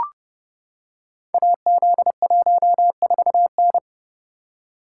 Morse code, heard as a keyed tone, about 30 wpm, 700 Hz. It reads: AZ14N
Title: Code Groups - 5 Letter:Numbers - 30wpm